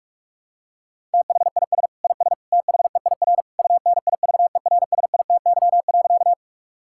Morse code, heard as a keyed tone, about 41 wpm, 700 Hz. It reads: THIS IS THEIR UNIVERSITY.
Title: Sentences from Top 1000 Words - Review Part 5 - 45wpm